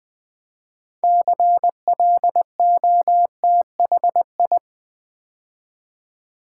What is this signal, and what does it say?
Morse code, heard as a keyed tone, about 20 wpm, 700 Hz. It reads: CLOTHI
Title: Build A Word - 2 Letters At A Time - 8-Letter Words - 20wpm